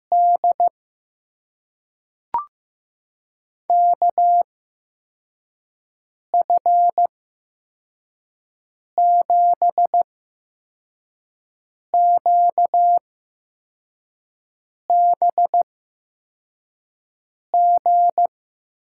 Morse code, heard as a keyed tone, about 15 wpm, spaced as for 4 wpm, 700 Hz. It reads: D KF7QBG
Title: USA Call Signs - 6 Characters - 8x Character Spacing - 15wpm